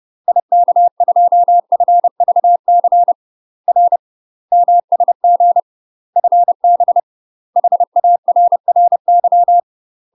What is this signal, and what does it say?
Morse code, heard as a keyed tone, about 30 wpm, 700 Hz. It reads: IK2FVC R MSG FB HARRY
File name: UK MASTER QSOs TRAINING / 30WPM / UK MASTER QSOs 0011